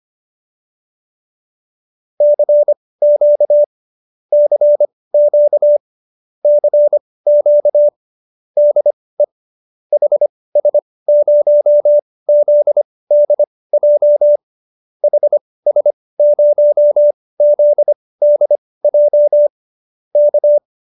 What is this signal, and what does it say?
Morse code, heard as a keyed tone, about 25 wpm, spaced as for 18 wpm, 590 Hz. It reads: CQ CQ CQ DE HS0ZDJ HS0ZDJ K